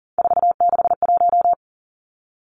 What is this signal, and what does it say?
Morse code, heard as a keyed tone, about 40 wpm, 700 Hz. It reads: 461
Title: Numbers - 3-Digit 40wpm